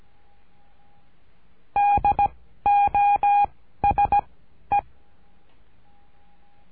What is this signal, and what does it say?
Morse code, heard as a keyed tone, about 17 wpm, 820 Hz. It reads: D O S E
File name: cursos / CURSO DE TELEGRAFIA / Telegrafia - Trilha 13